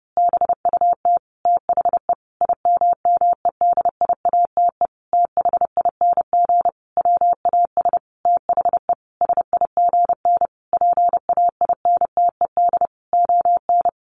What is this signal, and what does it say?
Morse code, heard as a keyed tone, about 30 wpm, 700 Hz. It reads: BUT THE IMMEDIATE THING WAS THE SIGN PAINTED ON
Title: A Sound Of Thunder 30wpm